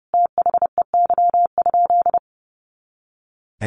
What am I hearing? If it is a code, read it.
Morse code, heard as a keyed tone, about 30 wpm, 700 Hz. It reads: THEY?
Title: Sentences from Top 300 Words 30wpm